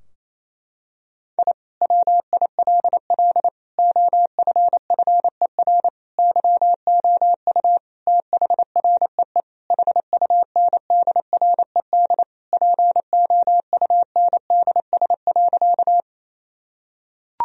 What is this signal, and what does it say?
Morse code, heard as a keyed tone, about 28 wpm, 700 Hz. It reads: I WILL OFFER YOU THREE HUNDRED POUNDS.
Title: Sentences from Top 1000 Words - Review Part 4 - 28wpm